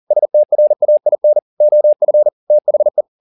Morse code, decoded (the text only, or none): STRAIN OF THE